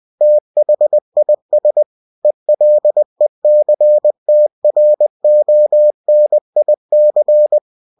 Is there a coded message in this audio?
Yes: Morse code from 0.2 to 7.6 s, heard as a keyed tone, about 20 wpm, 600 Hz. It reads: THIS ELECTRONIC